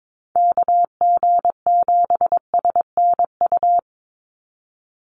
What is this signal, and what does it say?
Morse code, heard as a keyed tone, about 22 wpm, 700 Hz. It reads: KG7SNU
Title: USA Call Signs 22wpm